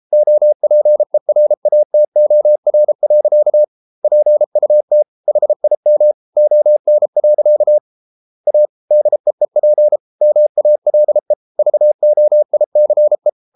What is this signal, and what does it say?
Morse code, heard as a keyed tone, about 33 wpm, 600 Hz. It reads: OPERATOR. PUT HIM ON. A DEEP MALE VOICE